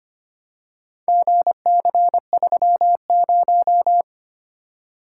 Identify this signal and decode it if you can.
Morse code, heard as a keyed tone, about 25 wpm, 700 Hz. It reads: GC30
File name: practice / 25wpm / Four Letter-Number Combinations - 25wpm